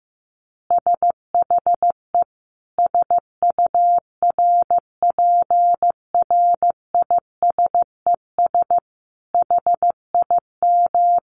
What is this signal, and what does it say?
Morse code, heard as a keyed tone, about 15 wpm, 700 Hz. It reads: SHE SURPRISES HIM
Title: Additional Easy Phrases 15wpm